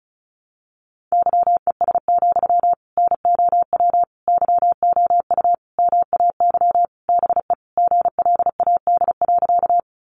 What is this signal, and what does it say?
Morse code, heard as a keyed tone, about 34 wpm, 700 Hz. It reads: YES, NOW YOU MAY BE GLAD.
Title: Sentences from Top 800 Words 35wpm